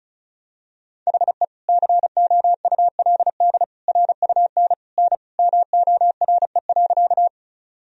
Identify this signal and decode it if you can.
Morse code, heard as a keyed tone, about 36 wpm, 700 Hz. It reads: HE COULD RUN N MORE.
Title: Sentences from Top 300 Words and Abbreviations 35wpm